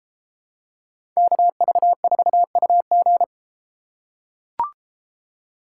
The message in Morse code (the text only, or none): KV4UG